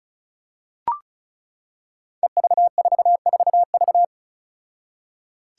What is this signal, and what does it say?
Morse code, heard as a keyed tone, about 36 wpm, 700 Hz. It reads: EV44V
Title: Five Challenging Letter-Number Combinations 35wpm